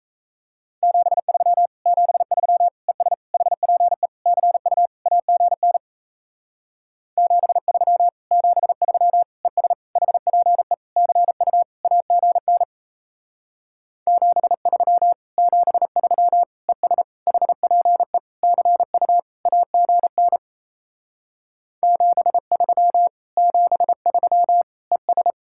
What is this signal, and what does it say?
Morse code, heard as a keyed tone, about 37 wpm, 700 Hz. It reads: 73 73 ES HPE CU AGN 73 73 ES HPE CU AGN 73 73 ES HPE CU AGN 73 73 ES